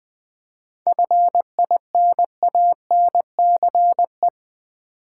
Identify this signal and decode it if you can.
Morse code, heard as a keyed tone, about 20 wpm, 700 Hz. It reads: FINANCE
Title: Seven Letter Words - 20wpm